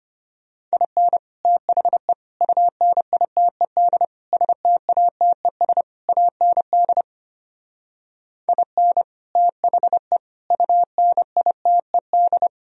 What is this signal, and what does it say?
Morse code, heard as a keyed tone, about 32 wpm, 700 Hz. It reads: IN THE UNITED STATES AND IN THE UNITED